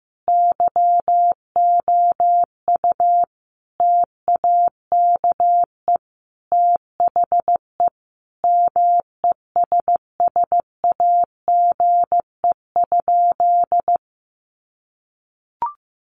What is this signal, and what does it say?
Morse code, heard as a keyed tone, about 15 wpm, 700 Hz. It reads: YOU TAKE THE MESSAGE?
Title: Sentences from Top 1100 Words 15wpm